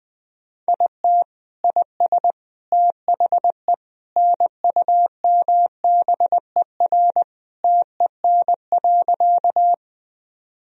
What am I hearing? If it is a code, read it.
Morse code, heard as a keyed tone, about 20 wpm, 700 Hz. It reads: IT IS THE NUMBER TEN.
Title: Sentences from Top 1000 Words - Review Part 2 - 20wpm